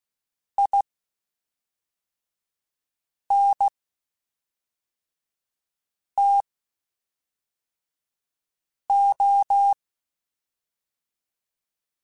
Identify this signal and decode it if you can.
Morse code, heard as a keyed tone, about 16 wpm, spaced as for 3 wpm, 800 Hz. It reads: INTO
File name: (spaced) / QSOs Spaced-mp3s / kc7haa @5 sp